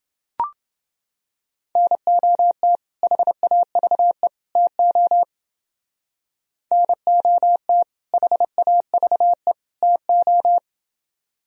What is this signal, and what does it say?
Morse code, heard as a keyed tone, about 29 wpm, 700 Hz. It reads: NOT HAVE TO NOT HAVE TO